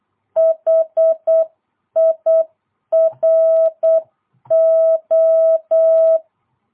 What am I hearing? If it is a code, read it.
Morse code, heard as a keyed tone, about 8 wpm, 660 Hz. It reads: HIRO